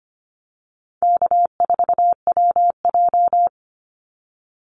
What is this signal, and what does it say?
Morse code, heard as a keyed tone, about 25 wpm, 700 Hz. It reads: K4WJ
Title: Easy Call Signs - 25wpm